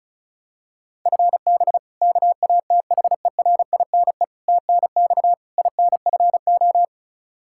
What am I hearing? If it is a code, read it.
Morse code, heard as a keyed tone, about 36 wpm, 700 Hz. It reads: FB KATHERINE TNX INFO
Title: K1USN SST QSO Practice 35wpm